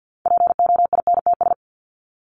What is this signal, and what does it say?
Morse code, heard as a keyed tone, about 48 wpm, 700 Hz. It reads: POINTS
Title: ES:S Suffix - Top 100 Words with Base Words 50wpm